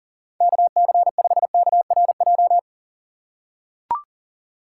Morse code, heard as a keyed tone, about 40 wpm, 700 Hz. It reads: KC5KRJ
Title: USA Call Signs - 6 Characters - 3x Character Spacing - 40wpm